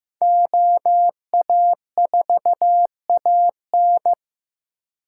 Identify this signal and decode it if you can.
Morse code, heard as a keyed tone, about 18 wpm, 700 Hz. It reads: OA4AN
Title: Speed Racing - USA 2x2 Call Signs 15wpm